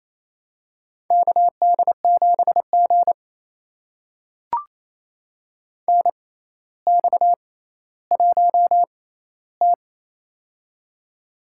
Morse code, heard as a keyed tone, about 28 wpm, spaced as for 14 wpm, 700 Hz. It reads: KD7G NX1T
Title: USA Call Signs - 4 Characters - 6x Character Spacing - 28wpm